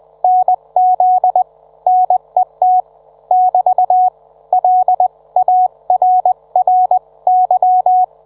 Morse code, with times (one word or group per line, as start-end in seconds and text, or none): 0.2-1.4 NZ
1.9-2.8 NET
3.3-4.1 <BT>
4.5-8.0 LARRY